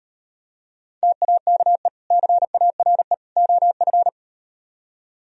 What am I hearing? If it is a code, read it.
Morse code, heard as a keyed tone, about 38 wpm, 700 Hz. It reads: TAKE CARE OF